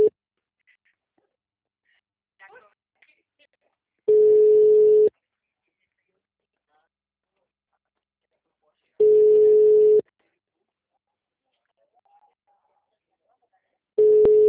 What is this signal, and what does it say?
Ringback tone (Europe, 425 Hz), with speech.